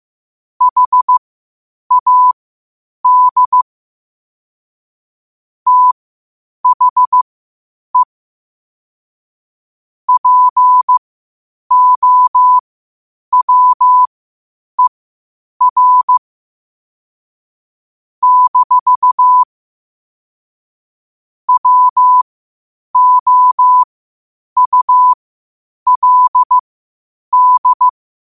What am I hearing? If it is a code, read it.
Morse code, heard as a keyed tone, about 15 wpm, spaced as for 8 wpm, 1000 Hz. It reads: HAD THE POWER - WOULD
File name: qotdfiles / 8wpm